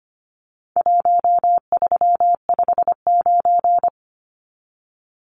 Morse code, heard as a keyed tone, about 25 wpm, 700 Hz. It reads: 1359
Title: Numbers - Contest Style - 4-Digit 25wpm